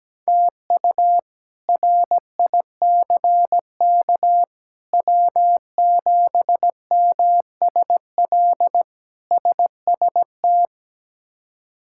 Morse code, heard as a keyed tone, about 17 wpm, 700 Hz. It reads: TU RICK W7MSL SST
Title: K1USN SST QSO Practice - 17wpm